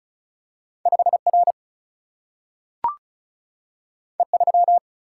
Morse code, heard as a keyed tone, about 36 wpm, 700 Hz. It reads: 5R E3
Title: Single-Letter Single-Number 35wpm